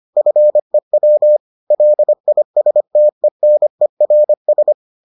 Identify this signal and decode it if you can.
Morse code, heard as a keyed tone, about 25 wpm, 600 Hz. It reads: FEW LISTENERS